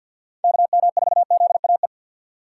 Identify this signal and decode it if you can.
Morse code, heard as a keyed tone, about 53 wpm, 700 Hz. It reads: KM4ZAE
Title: USA Call Signs - 6 Characters - 6x Character Spacing - 50wpm